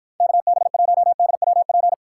Morse code, heard as a keyed tone, about 53 wpm, 700 Hz. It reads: KB1DWP